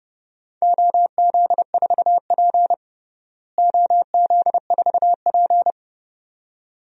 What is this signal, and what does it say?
Morse code, heard as a keyed tone, about 30 wpm, 700 Hz. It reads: OZ4P OZ4P
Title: DXCC-100 Call Signs - 4 Characters - Repeated 2x - 30wpm